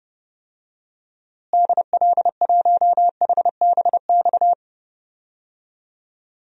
Morse code, heard as a keyed tone, about 30 wpm, 700 Hz. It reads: DL1HBX
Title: DXCC-100 Call Signs - 6 Characters - Repeated 2x - 30wpm